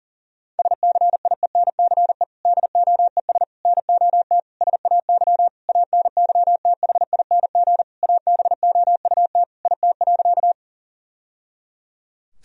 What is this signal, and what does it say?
Morse code, heard as a keyed tone, about 40 wpm, 700 Hz. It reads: SCIENCE DOES NOT SAY ANYTHING ABOUT IT.
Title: Sentences from Top 1000 Words - Review Part 2 - 40wpm